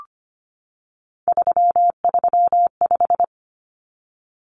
Morse code, heard as a keyed tone, about 25 wpm, 700 Hz. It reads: 335
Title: Numbers - 3-Digit 25wpm